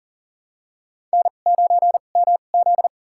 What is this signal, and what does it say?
Morse code, heard as a keyed tone, about 40 wpm, spaced as for 29 wpm, 700 Hz. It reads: N9MZ
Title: USA Call Signs - 4 Characters - 2x Character Spacing - 40wpm